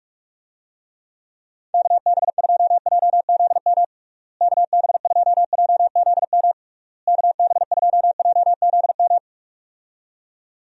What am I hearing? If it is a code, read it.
Morse code, heard as a keyed tone, about 46 wpm, 700 Hz. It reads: KB2JZM KB2JZM KB2JZM
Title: USA Call Signs - 6 Characters - Repeated 3x - 45wpm